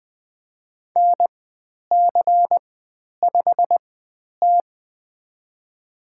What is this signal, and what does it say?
Morse code, heard as a keyed tone, about 20 wpm, spaced as for 10 wpm, 700 Hz. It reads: NC5T
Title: 109 Lesson 18g - Letter m - Call Signs 20wpm